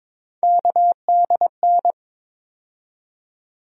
Morse code, heard as a keyed tone, about 21 wpm, 700 Hz. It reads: KDN
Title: USA Call Sign Postfixes - 3 Letters 22wpm